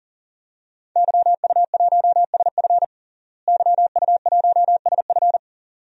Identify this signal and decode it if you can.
Morse code, heard as a keyed tone, about 40 wpm, 700 Hz. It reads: YU1SF YU1SF
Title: DXCC-100 Call Signs - 5 Characters - Repeated 2x - 40wpm